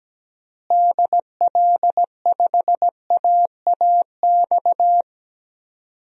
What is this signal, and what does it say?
Morse code, heard as a keyed tone, about 17 wpm, 700 Hz. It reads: DL5AAX
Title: DXCC-100 Call Signs - 6 Characters - Repeated 2x - 17wpm